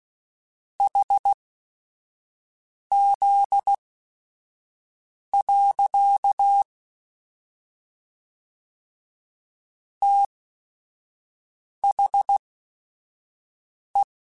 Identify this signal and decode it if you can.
Morse code, heard as a keyed tone, about 16 wpm, spaced as for 5 wpm, 800 Hz. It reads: HZ. THE
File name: Technical Subjects / Technical Subjects-mp3s / XL @5